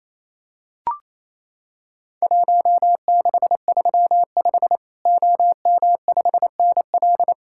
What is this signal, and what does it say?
Morse code, heard as a keyed tone, about 27 wpm, 700 Hz. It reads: 1635 OM5NL
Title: Worked All Europe DX Contest - QTC Exchange - 28wpm